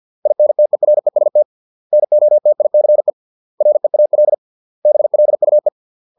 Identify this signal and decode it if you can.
Morse code, heard as a keyed tone, about 50 wpm, 600 Hz. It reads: INTEREST NOTICE REAL BLUE